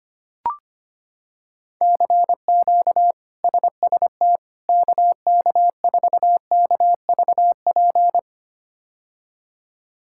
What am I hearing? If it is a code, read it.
Morse code, heard as a keyed tone, about 25 wpm, 700 Hz. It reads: CQ SST KK4KVP
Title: K1USN SST QSO Practice - 25wpm